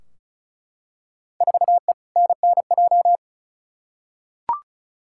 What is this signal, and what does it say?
Morse code, heard as a keyed tone, about 34 wpm, 700 Hz. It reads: 4E NNJ
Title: Field Day Exchanges - 35wpm